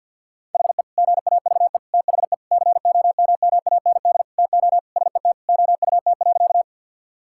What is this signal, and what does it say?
Morse code, heard as a keyed tone, about 50 wpm, 700 Hz. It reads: HE GAVE THE COMMAND TO SET OUT.
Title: Sentences from Top 1600 Words 50wpm